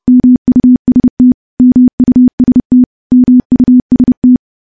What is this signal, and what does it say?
Morse code, heard as a keyed tone, about 31 wpm, 250 Hz. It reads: MUST MUST MUST
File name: morse / morse-training-30WPM